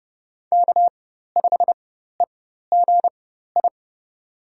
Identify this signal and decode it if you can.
Morse code, heard as a keyed tone, about 30 wpm, spaced as for 14 wpm, 700 Hz. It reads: K5EGI
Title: USA Call Signs - 5 Characters - 4x Character Spacing - 30wpm